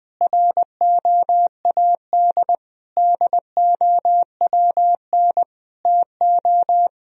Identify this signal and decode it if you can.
Morse code, heard as a keyed tone, about 20 wpm, 700 Hz. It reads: ROAD DOWN TO
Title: Sentences from Top 1000 Words - Review Part 4 - 20wpm